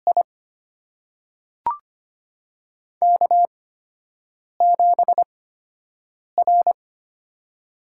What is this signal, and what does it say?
Morse code, heard as a keyed tone, about 25 wpm, spaced as for 7 wpm, 700 Hz. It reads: I K7R
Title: USA Call Signs - 8x Character Spacing - 25wpm